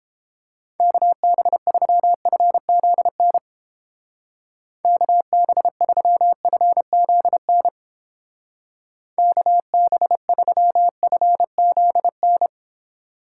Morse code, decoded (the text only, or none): KB3FZN KB3FZN KB3FZN